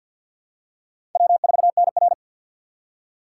Morse code, heard as a keyed tone, about 48 wpm, 700 Hz. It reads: W4NR
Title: USA Call Signs - 2x Character Spacing - 50wpm